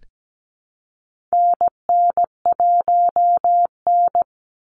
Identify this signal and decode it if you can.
Morse code, heard as a keyed tone, about 17 wpm, 700 Hz. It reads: NN1N
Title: Call Signs Repeated 3x - 17wpm